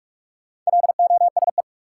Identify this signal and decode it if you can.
Morse code, heard as a keyed tone, about 46 wpm, 700 Hz. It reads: LOSE
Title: Sentences from Top 1100 Words - New Word Drill - 45wpm